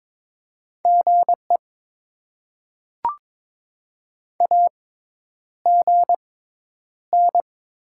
Morse code, heard as a keyed tone, about 22 wpm, spaced as for 10 wpm, 700 Hz. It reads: GE AGN
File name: Top 20 QSO Elements - 6x Character Spacing - 22wpm